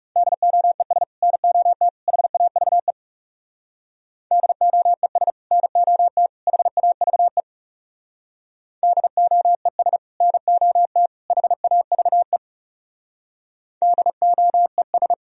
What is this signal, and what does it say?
Morse code, heard as a keyed tone, about 39 wpm, 700 Hz. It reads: DOES NOT HAVE DOES NOT HAVE DOES NOT HAVE DOES